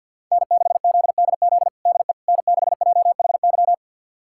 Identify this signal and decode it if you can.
Morse code, heard as a keyed tone, about 50 wpm, 700 Hz. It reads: N6ZDZ DE N6JHY
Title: USA Callsign DE Callsign - 5 Characters - 50wpm